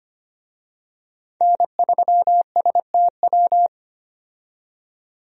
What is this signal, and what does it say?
Morse code, heard as a keyed tone, about 25 wpm, 700 Hz. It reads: N3STW
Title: USA Call Signs - 5 Characters - 25wpm